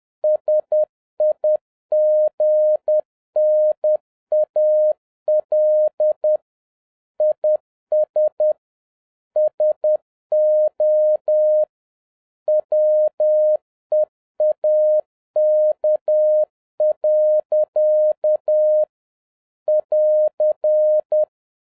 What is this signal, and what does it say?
Morse code, heard as a keyed tone, about 10 wpm, 610 Hz. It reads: SIGNAL IS SO WEAK. <AR>
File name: 2011-07-mp3 / n41 EXCITING QRP CW OPERATION 10wpm